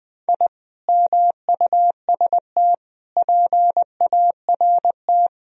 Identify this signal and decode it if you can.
Morse code, heard as a keyed tone, about 20 wpm, 700 Hz. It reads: I MUST PART
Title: Sentences from Top 1000 Words - Review Part 1 - 20wpm